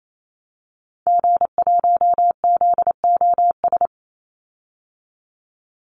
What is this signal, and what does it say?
Morse code, heard as a keyed tone, about 28 wpm, 700 Hz. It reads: G1ZOS
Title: Call Signs Repeated 2x - 28wpm